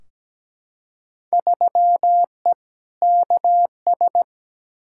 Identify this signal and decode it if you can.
Morse code, heard as a keyed tone, about 17 wpm, 700 Hz. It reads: 3E KS
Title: Field Day Exchanges Encore - 17wpm